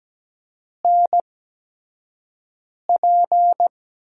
Morse code, heard as a keyed tone, about 17 wpm, spaced as for 5 wpm, 700 Hz. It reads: NP
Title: USA Call Signs - 4 Characters - 8x Character Spacing - 17wpm